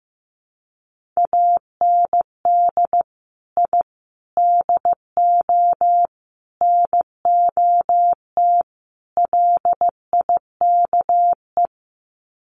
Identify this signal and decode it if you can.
Morse code, heard as a keyed tone, about 15 wpm, 700 Hz. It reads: AND I DO NOT LIKE
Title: Sets of 5 Words Encore2 15wpm